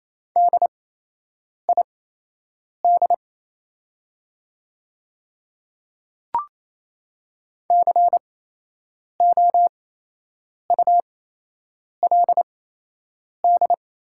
Morse code, heard as a keyed tone, about 28 wpm, spaced as for 8 wpm, 700 Hz. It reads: DID COULD
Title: 8x Character Spacing - Top 100 Words - 28wpm